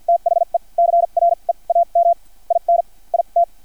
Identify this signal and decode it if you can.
Morse code, heard as a keyed tone, about 50 wpm, spaced as for 38 wpm, 680 Hz. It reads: THE CREAM IN IT
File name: archived / CRSnet